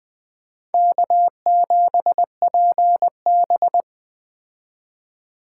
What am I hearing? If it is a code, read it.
Morse code, heard as a keyed tone, about 20 wpm, 700 Hz. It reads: K7PB
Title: Speed Racing - USA 1x2 Call Signs 17wpm